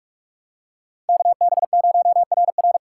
Morse code, heard as a keyed tone, about 47 wpm, 700 Hz. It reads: KB0RR
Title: USA Call Signs - 5 Characters - 2x Character Spacing - 45wpm